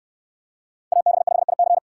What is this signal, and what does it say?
Morse code, heard as a keyed tone, about 65 wpm, 700 Hz. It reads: AB5IC